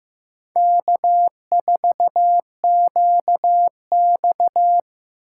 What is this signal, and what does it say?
Morse code, heard as a keyed tone, about 18 wpm, 700 Hz. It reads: K4QX
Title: Speed Racing - USA 1x2 Call Signs 15wpm